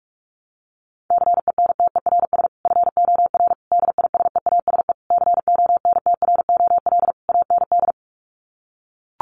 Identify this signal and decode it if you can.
Morse code, heard as a keyed tone, about 44 wpm, 700 Hz. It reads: CENTERS FOR DISEASE CONTROL AND
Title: Sets of 5 Words Encore2 45wpm